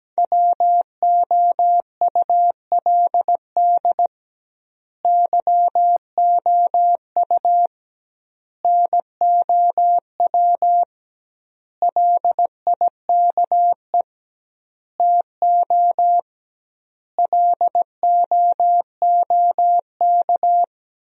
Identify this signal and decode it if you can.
Morse code, heard as a keyed tone, about 17 wpm, 700 Hz. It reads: WOULD YOU NOW LIKE TO LOOK
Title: 2x Word-Spacing - Sentences from Top 200 Words - 17wpm